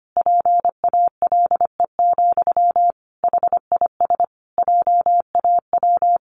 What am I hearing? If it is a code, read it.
Morse code, heard as a keyed tone, about 25 wpm, 700 Hz. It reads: PALE, HIS JAW
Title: A Sound Of Thunder 25wpm